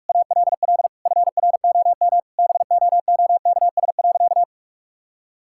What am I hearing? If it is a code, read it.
Morse code, heard as a keyed tone, about 41 wpm, 700 Hz. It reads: ALL FROM BOOKS.
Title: Sentences from Top 1000 Words - Review Part 6 - 45wpm